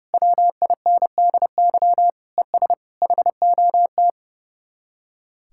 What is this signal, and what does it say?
Morse code, heard as a keyed tone, about 36 wpm, 700 Hz. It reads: WINDY ES HOT
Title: Speed-Racing Common QSO Phrases - 30wpm